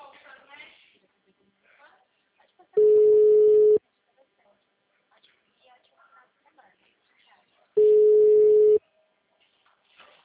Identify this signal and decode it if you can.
Ringback tone (Europe, 425 Hz), with speech.